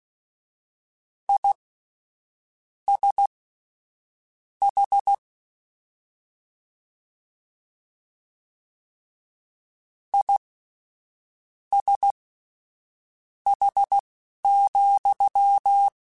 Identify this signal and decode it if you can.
Morse code, heard as a keyed tone, about 16 wpm, spaced as for 6 wpm, 800 Hz. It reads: ISH ISH,
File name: General / General-mp3s / Suffixes sp @10